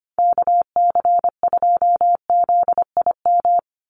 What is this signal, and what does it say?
Morse code, heard as a keyed tone, about 25 wpm, 700 Hz. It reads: KC2ZIM
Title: USA Call Signs 25wpm